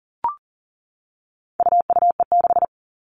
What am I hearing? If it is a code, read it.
Morse code, heard as a keyed tone, about 40 wpm, 700 Hz. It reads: UUE6
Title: All Cut Numbers - 4-Digit 40wpm